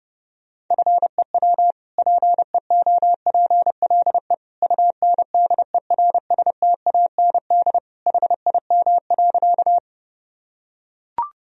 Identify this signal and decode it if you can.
Morse code, heard as a keyed tone, about 28 wpm, 700 Hz. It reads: FEW PEOPLE UNDERSTAND HIM.
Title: Sentences from Top 1000 Words - Review Part 1 - 30wpm